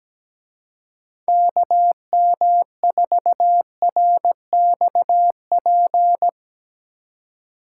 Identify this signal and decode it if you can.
Morse code, heard as a keyed tone, about 18 wpm, 700 Hz. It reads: KM4RXP
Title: Speed Racing - USA 2x3 Call Signs 17wpm